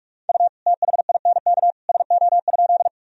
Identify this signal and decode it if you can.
Morse code, heard as a keyed tone, about 41 wpm, 700 Hz. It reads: U THINK SO?
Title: Sentences from Top 200 Words and Abbreviations 45wpm